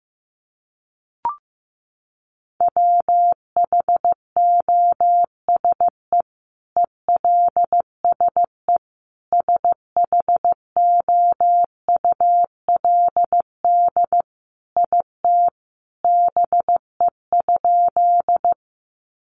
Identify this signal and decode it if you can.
Morse code, heard as a keyed tone, about 15 wpm, 700 Hz. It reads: WHOSE ELSE SHOULD IT BE?
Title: Sentences from Top 1200 Words 15wpm